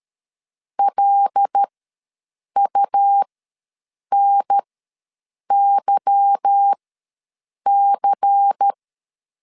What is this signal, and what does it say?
Morse code, heard as a keyed tone, about 13 wpm, spaced as for 7 wpm, 800 Hz. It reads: LUNYC
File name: LEZIONI / 9 2867 ws 12wpm 3431